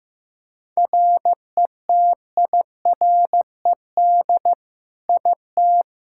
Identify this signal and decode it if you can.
Morse code, heard as a keyed tone, about 16 wpm, 700 Hz. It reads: RETIRED IT
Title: Practice QSO - Exchange 3 - More Info - 15wpm